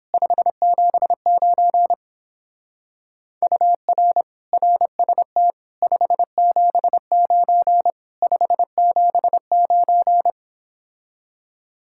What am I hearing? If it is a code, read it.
Morse code, heard as a keyed tone, about 26 wpm, 700 Hz. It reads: 579 UR RST 579 579
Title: Speed-Racing Common QSO Phrases Encore - 22wpm